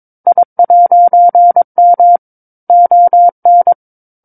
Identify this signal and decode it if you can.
Morse code, heard as a keyed tone, about 22 wpm, 700 Hz. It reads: I'M ON
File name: 06-Chapter-06 22 WPM